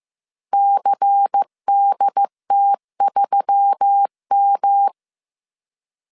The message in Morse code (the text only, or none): CDT3M